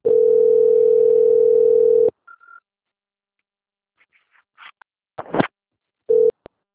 Ringback tone (North America), 440 + 480 Hz.